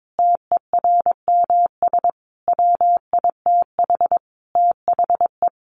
Morse code, heard as a keyed tone, about 22 wpm, 700 Hz. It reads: TERMS WITH THE
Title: Sets of 5 Words Encore2 22wpm